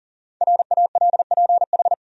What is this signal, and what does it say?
Morse code, heard as a keyed tone, about 40 wpm, 700 Hz. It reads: RALPH
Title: USA Common Names - 40wpm